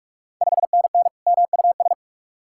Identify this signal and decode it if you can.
Morse code, heard as a keyed tone, about 42 wpm, 700 Hz. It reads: 5NN MUS
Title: Ontario QSO Party Exchange 45wpm